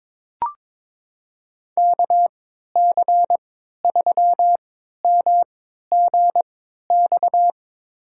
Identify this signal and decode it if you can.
Morse code, heard as a keyed tone, about 22 wpm, spaced as for 13 wpm, 700 Hz. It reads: KC3MGX
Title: USA Call Signs - 6 Characters - 3x Character Spacing - 22wpm